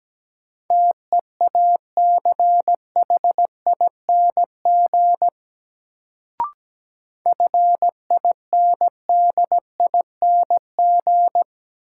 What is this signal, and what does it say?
Morse code, heard as a keyed tone, about 17 wpm, 700 Hz. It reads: TEACHING FINDING
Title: Sentences from Top 1100 Words - New Word Drill - 17wpm